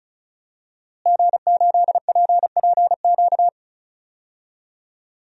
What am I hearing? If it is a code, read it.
Morse code, heard as a keyed tone, about 36 wpm, 700 Hz. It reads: G8PPQ
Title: DXCC-100 Call Signs - 5 Characters - Repeated 3x - 35wpm